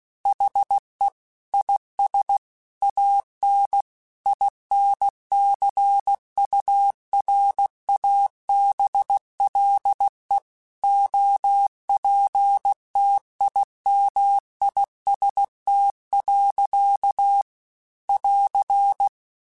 Morse code, heard as a keyed tone, about 16 wpm, 800 Hz. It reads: HE IS AN INCURABLE OPTIMIST. <AR>